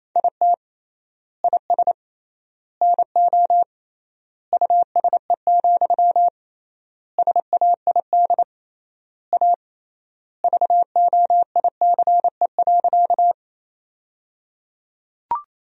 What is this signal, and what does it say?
Morse code, heard as a keyed tone, about 28 wpm, 700 Hz. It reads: IT IS NO USE, SAID A VOICE.
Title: 3x Word-Spacing - Sentences from Top 500 Words - 28wpm